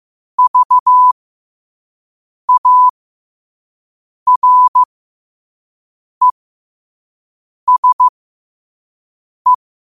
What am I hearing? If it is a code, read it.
Morse code, heard as a keyed tone, about 15 wpm, spaced as for 5 wpm, 1000 Hz. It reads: VARESE
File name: qotdfiles / 5wpm